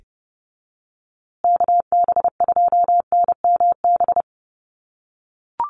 Morse code, heard as a keyed tone, about 30 wpm, 700 Hz. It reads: KB2NMB
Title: USA Call Signs Repeated 2x 30wpm